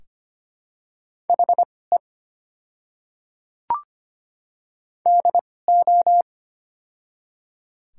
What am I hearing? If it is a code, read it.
Morse code, heard as a keyed tone, about 25 wpm, spaced as for 18 wpm, 700 Hz. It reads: HE DO
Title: 2x Character Spacing - Top 300 Words - 25wpm